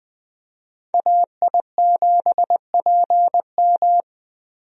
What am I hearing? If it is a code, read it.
Morse code, heard as a keyed tone, about 20 wpm, 700 Hz. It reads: AI7PM
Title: USA Call Signs - 5 Characters - Repeated 3x - 20wpm